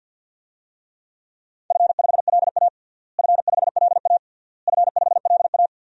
Morse code, heard as a keyed tone, about 52 wpm, 700 Hz. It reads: F5LA F5LA F5LA
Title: DXCC-100 Call Signs - 4 Characters - Repeated 3x - 50wpm